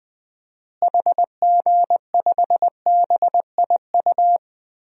Morse code, heard as a keyed tone, about 20 wpm, 700 Hz. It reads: HG5BIU
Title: DXCC-100 Call Signs - 6 Characters - Repeated 3x - 20wpm